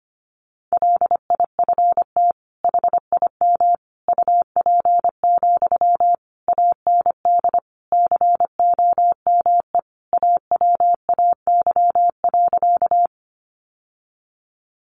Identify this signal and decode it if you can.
Morse code, heard as a keyed tone, about 25 wpm, 700 Hz. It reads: LIFT HIM UP, AND COME AWAY.
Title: Sentences from Top 1000 Words 25wpm